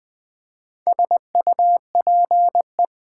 Morse code, heard as a keyed tone, about 20 wpm, 700 Hz. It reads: SUPE
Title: Build A Word - 2 Letters At A Time - 6-Letter Words - 20wpm